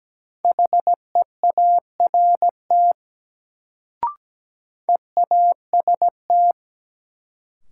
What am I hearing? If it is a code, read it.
Morse code, heard as a keyed tone, about 17 wpm, 700 Hz. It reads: HEART EAST
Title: Sets of 3 Words - New Word Drill - 17wpm